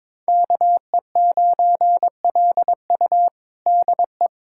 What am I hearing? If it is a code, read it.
Morse code, heard as a keyed tone, about 22 wpm, 700 Hz. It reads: KE9LU DE
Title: USA Callsign DE Callsign - 5 Characters - 22wpm